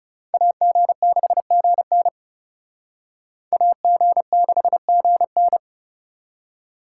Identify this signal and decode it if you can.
Morse code, heard as a keyed tone, about 38 wpm, 700 Hz. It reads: AG6GN AG6GN
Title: Speed Racing - USA 2x2 Call Signs 30wpm